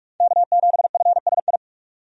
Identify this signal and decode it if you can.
Morse code, heard as a keyed tone, about 42 wpm, 700 Hz. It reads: K7FSI
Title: USA Call Signs - 5 Characters - 45wpm